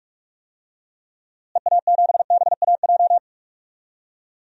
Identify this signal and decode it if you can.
Morse code, heard as a keyed tone, about 46 wpm, 700 Hz. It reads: EA7BAJ
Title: DXCC-100 Call Signs - 6 Characters - Repeated 3x - 45wpm